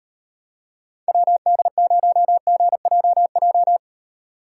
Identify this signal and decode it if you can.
Morse code, heard as a keyed tone, about 32 wpm, 700 Hz. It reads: WD0GJJ